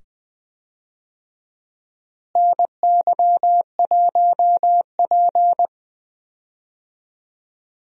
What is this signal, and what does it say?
Morse code, heard as a keyed tone, about 20 wpm, 700 Hz. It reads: NY1P
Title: USA Call Signs - 4 Characters - Repeated 2x - 20wpm